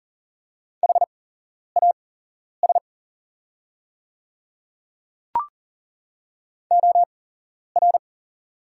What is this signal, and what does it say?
Morse code, heard as a keyed tone, about 40 wpm, spaced as for 11 wpm, 700 Hz. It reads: HAS OR
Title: 8x Character Spacing - Top 100 Words - 40wpm